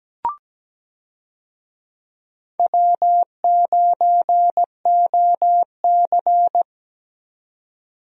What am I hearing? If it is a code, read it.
Morse code, heard as a keyed tone, about 17 wpm, 700 Hz. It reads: W9OC
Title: USA Call Signs - 4 Characters - 17wpm